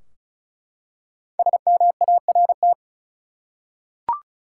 Morse code, heard as a keyed tone, about 34 wpm, 700 Hz. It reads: SMART
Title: Five Letter Words - 35wpm